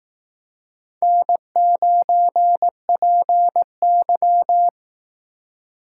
Morse code, heard as a keyed tone, about 19 wpm, 700 Hz. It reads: N9PY
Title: Speed Racing - USA 1x2 Call Signs 15wpm